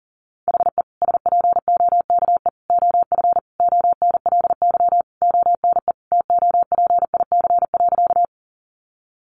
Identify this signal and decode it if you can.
Morse code, heard as a keyed tone, about 40 wpm, 700 Hz. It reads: HE SPOKE OF ONLY ONE TOPIC.
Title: Sentences from Top 1200 Words 40wpm